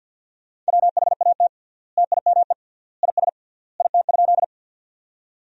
Morse code, heard as a keyed tone, about 52 wpm, 700 Hz. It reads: WHAT TIME IS IT?
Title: 3x Word-Spacing - Sentences from Top 100 Words - 50wpm